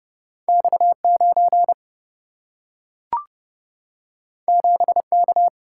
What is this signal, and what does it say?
Morse code, heard as a keyed tone, about 30 wpm, 700 Hz. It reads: X 9 7 K
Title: Single-Letter Single-Number 30wpm